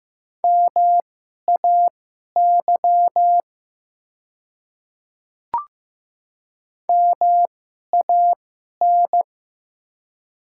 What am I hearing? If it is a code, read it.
Morse code, heard as a keyed tone, about 15 wpm, spaced as for 11 wpm, 690 Hz. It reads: MAY MAN
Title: 2x Character Spacing - Top 200 Words - 15wpm